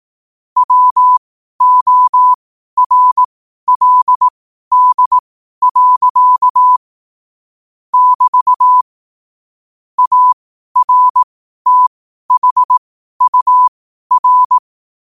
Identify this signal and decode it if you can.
Morse code, heard as a keyed tone, about 18 wpm, spaced as for 12 wpm, 1000 Hz. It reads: WORLD. <BT> ARTHUR
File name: qotdfiles / 12wpm